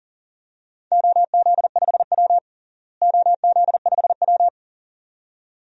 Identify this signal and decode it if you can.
Morse code, heard as a keyed tone, about 37 wpm, 700 Hz. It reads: OZ5W OZ5W
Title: DXCC-100 Call Signs - 4 Characters - Repeated 2x - 40wpm